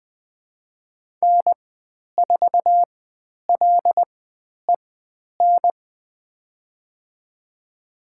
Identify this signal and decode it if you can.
Morse code, heard as a keyed tone, about 20 wpm, spaced as for 10 wpm, 700 Hz. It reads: N4LEN
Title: 095 Lesson 16g - Letter c - Call Signs 20wpm